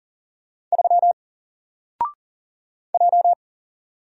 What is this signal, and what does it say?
Morse code, heard as a keyed tone, about 40 wpm, 700 Hz. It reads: 3 J
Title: Single Letter-Number Alt 40wpm